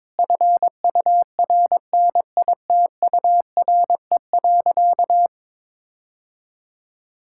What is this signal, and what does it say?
Morse code, heard as a keyed tone, about 22 wpm, 700 Hz. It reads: FURNITURE.
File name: practice / 22wpm / Sentences with 4-Word Phrases - 22wpm